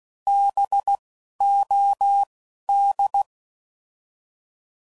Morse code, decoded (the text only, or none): B O D